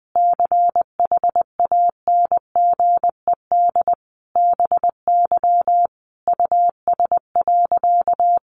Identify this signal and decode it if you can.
Morse code, heard as a keyed tone, about 20 wpm, 700 Hz. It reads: CHANGED BY US.
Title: A Sound Of Thunder 20wpm